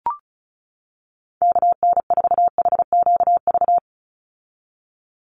Morse code, heard as a keyed tone, about 34 wpm, 700 Hz. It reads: KN4HQV
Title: USA Call Signs 35wpm